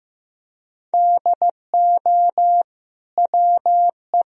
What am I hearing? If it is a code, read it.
Morse code, heard as a keyed tone, about 15 wpm, 700 Hz. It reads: DO WE
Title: Sentences from Top 1000 Words - Review Part 3 - 15wpm